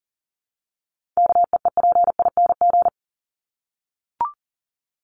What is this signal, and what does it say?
Morse code, heard as a keyed tone, about 40 wpm, 700 Hz. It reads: KEEPING
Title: ING Suffix - Top 100 Words with Base Words 40wpm